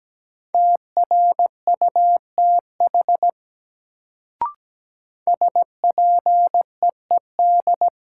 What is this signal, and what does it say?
Morse code, heard as a keyed tone, about 17 wpm, 700 Hz. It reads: TRUTH SPEED
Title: Sentences from Top 800 Words - New Word Drill - 17wpm